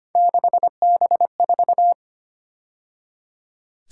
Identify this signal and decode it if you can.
Morse code, heard as a keyed tone, about 30 wpm, 700 Hz. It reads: 6B4
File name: Speed-Racing - Three Letter-Number Combinations - 25wpm